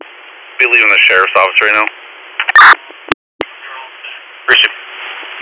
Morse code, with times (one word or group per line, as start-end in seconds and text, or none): none